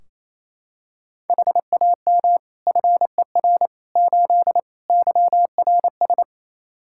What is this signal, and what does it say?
Morse code, heard as a keyed tone, about 27 wpm, 700 Hz. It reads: HAM FER 8 YRS
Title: Practice QSO - Exchange 3 - More Info - 28wpm